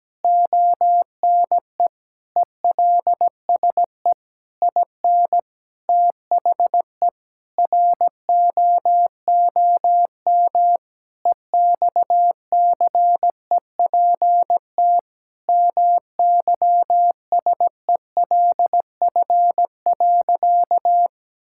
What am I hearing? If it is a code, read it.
Morse code, heard as a keyed tone, about 17 wpm, 700 Hz. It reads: ONE ELSE IN THE ROOM EXCEPT MYSELF.
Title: Sentences from Top 1600 Words 17wpm